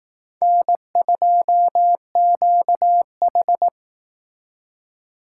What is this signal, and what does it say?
Morse code, heard as a keyed tone, about 18 wpm, 700 Hz. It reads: N2QH
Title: Speed Racing - USA Call Signs 15wpm